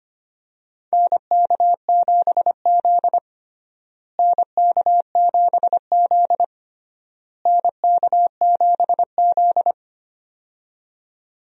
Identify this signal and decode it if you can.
Morse code, heard as a keyed tone, about 25 wpm, 700 Hz. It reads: NK7Z NK7Z NK7Z
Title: USA Call Signs - 4 Characters - Repeated 3x - 25wpm